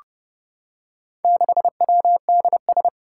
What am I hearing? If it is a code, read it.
Morse code, heard as a keyed tone, about 30 wpm, 700 Hz. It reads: 6WDS
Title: Four Challenging Letter-Number Combinations 30wpm